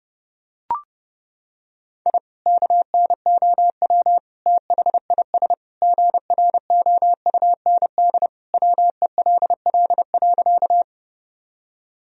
Morse code, heard as a keyed tone, about 30 wpm, 700 Hz. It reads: I KNOW THIS GROUND WELL.
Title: Sentences from Top 1000 Words - Review Part 2 - 30wpm